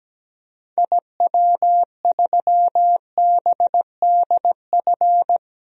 Morse code, heard as a keyed tone, about 17 wpm, 700 Hz. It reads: IW3BDF
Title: DXCC-100 Call Signs - 6 Characters - Repeated 3x - 17wpm